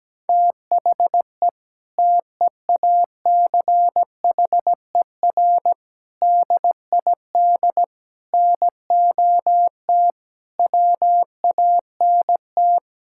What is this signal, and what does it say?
Morse code, heard as a keyed tone, about 17 wpm, 700 Hz. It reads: THE TEACHER DID NOT WANT
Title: Sentences from Top 1000 Words - Review Part 6 - 17wpm